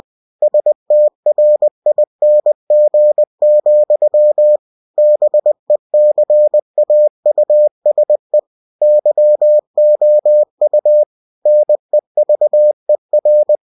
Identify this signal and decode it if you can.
Morse code, heard as a keyed tone, about 20 wpm, 600 Hz. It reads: STRING, BECAUSE YOU NEVER
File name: cw-books / winnie-the-pooh / winnie-20wpm-page-0109